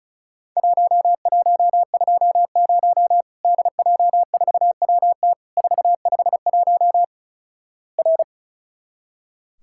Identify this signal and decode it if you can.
Morse code, heard as a keyed tone, about 36 wpm, 700 Hz. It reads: 1120 DJ4WT 451 R
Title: Worked All Europe DX Contest - QTC Exchange - 35wpm